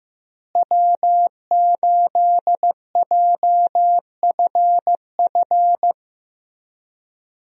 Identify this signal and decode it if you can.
Morse code, heard as a keyed tone, about 16 wpm, 700 Hz. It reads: W8JFF
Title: USA Call Signs - 5 Characters - Repeated 3x - 15wpm